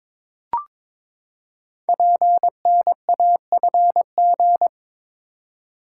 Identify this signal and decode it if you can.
Morse code, heard as a keyed tone, about 22 wpm, 700 Hz. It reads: PNAFG
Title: Code Groups - 5 Letters - 22wpm